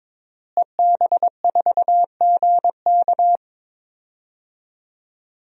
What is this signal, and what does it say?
Morse code, heard as a keyed tone, about 22 wpm, 700 Hz. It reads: EB4GK
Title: DXCC-100 Call Signs - 5 Characters - Repeated 2x - 22wpm